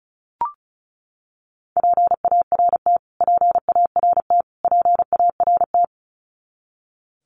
Morse code, heard as a keyed tone, about 34 wpm, 700 Hz. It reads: PART PART PART
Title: Top 200 Words Repeated 35wpm